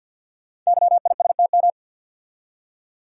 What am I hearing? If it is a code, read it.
Morse code, heard as a keyed tone, about 48 wpm, 700 Hz. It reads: YISTM
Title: Code Groups - 5 Letters - 50wpm